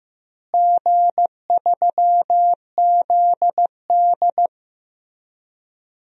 Morse code, heard as a keyed tone, about 16 wpm, 700 Hz. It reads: G3ZD
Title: DXCC-100 Call Signs - 4 Characters - Repeated 3x - 15wpm